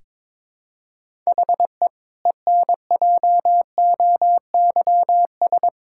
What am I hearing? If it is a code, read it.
Morse code, heard as a keyed tone, about 22 wpm, 700 Hz. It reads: HE ENJOYS
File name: Difficult Sentences with Hints - 22wpm